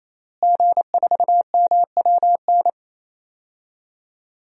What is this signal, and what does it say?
Morse code, heard as a keyed tone, about 28 wpm, 700 Hz. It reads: G4MWN
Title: DXCC-100 Call Signs - 5 Characters - Repeated 3x - 28wpm